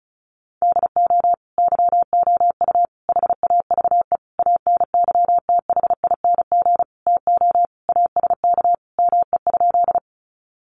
Morse code, heard as a keyed tone, about 34 wpm, 700 Hz. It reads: DO YOU HAVE ANYTHING TO ASK ME?
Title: Sentences from Top 1000 Words 35wpm